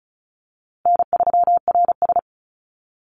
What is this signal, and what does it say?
Morse code, heard as a keyed tone, about 34 wpm, 700 Hz. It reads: N3RS
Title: Call Signs Repeated 3x - 35wpm